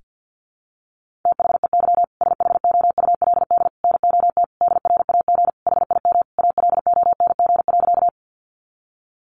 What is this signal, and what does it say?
Morse code, heard as a keyed tone, about 50 wpm, 700 Hz. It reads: THEY SHOULD NOT DRAG HIM ALONG.
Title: Sentences from Top 1500 Words 50wpm